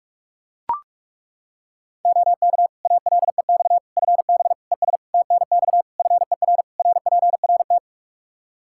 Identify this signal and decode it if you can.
Morse code, heard as a keyed tone, about 41 wpm, 700 Hz. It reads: OK ALEX FB ES TNX FER RPRT
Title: Practice QSO - Exchange 2 - Info - 45wpm